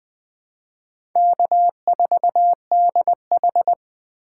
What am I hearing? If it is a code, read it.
Morse code, heard as a keyed tone, about 20 wpm, 700 Hz. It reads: K4DH
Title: USA Call Signs - 4 Characters - 3x Character Spacing - 20wpm